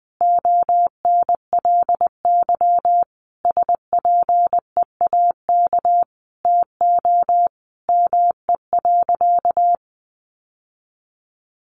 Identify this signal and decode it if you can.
Morse code, heard as a keyed tone, about 20 wpm, 700 Hz. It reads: ONLY SPEAK TO ME.
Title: Sentences from Top 1100 Words 20wpm